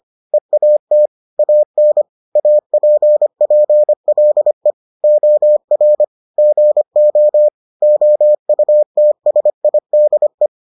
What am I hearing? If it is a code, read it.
Morse code, heard as a keyed tone, about 25 wpm, 600 Hz. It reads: EAT AN APPLE OR GO OUTSIDE